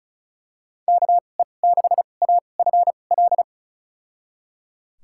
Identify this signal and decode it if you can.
Morse code, heard as a keyed tone, about 36 wpm, spaced as for 25 wpm, 700 Hz. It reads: KE6AFL
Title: USA Call Signs - 6 Characters - 2x Character Spacing - 35wpm